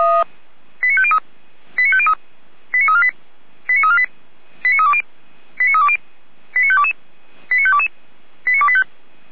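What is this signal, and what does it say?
Five-tone selective call (ZVEI), heard as quick runs of five tones, tones of about 70 ms. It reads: 89503 89503 89348 89348 89320 89320 89522 89522 89285
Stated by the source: ZVEI five-tone call 89220 received on an SDR, logged by feuerwehr-orlamuende.org (fire brigade Orlamünde).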